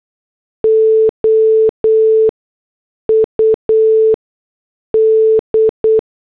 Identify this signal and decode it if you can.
Morse code, heard as a keyed tone, about 8 wpm, 440 Hz. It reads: O U D
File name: morse / headlines